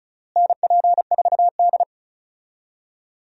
Morse code, heard as a keyed tone, about 36 wpm, 700 Hz. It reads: NP4D
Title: USA Call Signs - 4 Characters - Repeated 3x - 35wpm